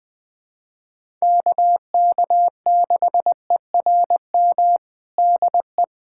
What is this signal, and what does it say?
Morse code, heard as a keyed tone, about 21 wpm, 700 Hz. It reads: KK6ERM DE
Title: USA Callsign DE Callsign - 6 Characters - 20wpm